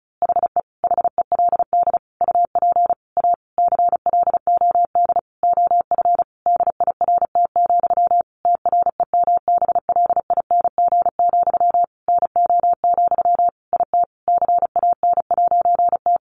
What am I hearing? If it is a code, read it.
Morse code, heard as a keyed tone, about 34 wpm, 700 Hz. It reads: HE HELD UP A CLOD OF DIRT, TREMBLING, NO, IT CAN'T
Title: A Sound Of Thunder 35wpm